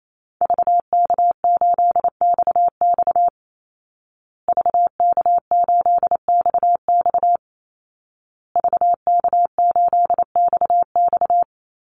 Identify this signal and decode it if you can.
Morse code, heard as a keyed tone, about 28 wpm, 700 Hz. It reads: VK8XX VK8XX VK8XX
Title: Call Signs Repeated 2x - 28wpm